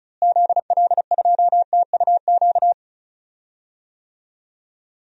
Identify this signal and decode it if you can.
Morse code, heard as a keyed tone, about 36 wpm, 700 Hz. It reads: ZL2TUQ
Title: DXCC-100 Call Signs - 6 Characters - Repeated 2x - 35wpm